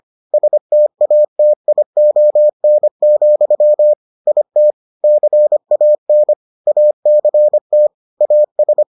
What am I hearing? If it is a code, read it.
Morse code, heard as a keyed tone, about 25 wpm, 600 Hz. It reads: STATION, IT CAN ACT AS